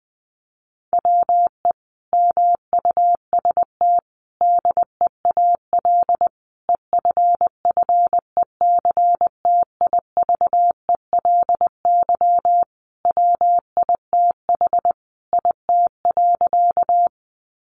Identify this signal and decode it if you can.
Morse code, heard as a keyed tone, about 19 wpm, 700 Hz. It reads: WE MUST DEAL EFFECTIVELY WITH IT.
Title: Sentences from Top 1400 Words 20wpm